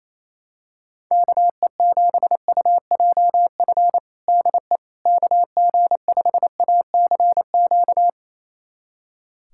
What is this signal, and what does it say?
Morse code, heard as a keyed tone, about 28 wpm, 700 Hz. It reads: KE7UJF DE KG5ACQ
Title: USA Callsign DE Callsign - 6 Characters - 28wpm